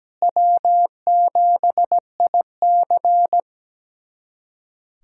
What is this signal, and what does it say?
Morse code, heard as a keyed tone, about 17 wpm, 700 Hz. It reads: W7IC
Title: USA Callsign DE Callsign - 4 Characters - 17wpm